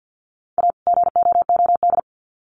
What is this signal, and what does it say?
Morse code, heard as a keyed tone, about 50 wpm, 700 Hz. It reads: A GOOD
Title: Sets of 2 Words 50wpm